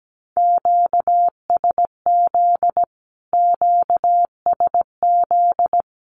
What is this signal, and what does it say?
Morse code, heard as a keyed tone, about 17 wpm, 700 Hz. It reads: QSZ QSZ
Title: Q-Codes Repeated 3x 17wpm